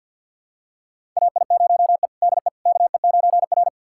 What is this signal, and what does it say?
Morse code, heard as a keyed tone, about 52 wpm, 700 Hz. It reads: AI0E DE KE9R
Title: USA Callsign DE Callsign - 4 Characters - 50wpm